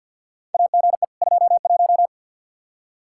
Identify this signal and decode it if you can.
Morse code, heard as a keyed tone, about 52 wpm, 700 Hz. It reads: AGE 21
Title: Practice QSO - Exchange 3 - More Info - 50wpm